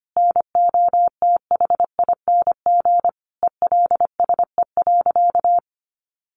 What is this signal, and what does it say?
Morse code, heard as a keyed tone, about 25 wpm, 700 Hz. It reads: NOTHING ELSE.
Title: Sentences from Top 800 Words 25wpm